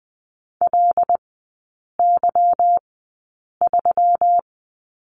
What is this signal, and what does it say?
Morse code, heard as a keyed tone, about 20 wpm, spaced as for 8 wpm, 690 Hz. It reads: LY3
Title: 145 Lesson 24e - Letter y - Three Characters 20wpm